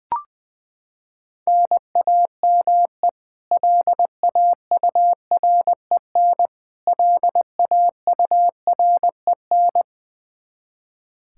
Morse code, heard as a keyed tone, about 21 wpm, 700 Hz. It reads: NAME LAUREN LAUREN
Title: Practice QSO - Exchange 1 - Beginning - 20wpm